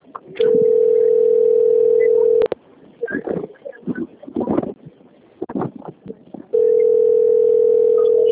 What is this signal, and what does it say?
Ringback tone (North America), 440 + 480 Hz.